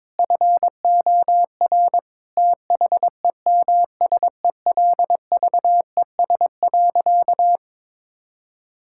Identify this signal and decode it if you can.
Morse code, heard as a keyed tone, about 22 wpm, 700 Hz. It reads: FOR THEMSELVES.
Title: Sentences from Top 1000 Words - Review Part 5 - 22wpm